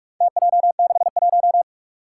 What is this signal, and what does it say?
Morse code, heard as a keyed tone, about 44 wpm, 700 Hz. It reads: TJ61
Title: Four Challenging Letter-Number Combinations 45wpm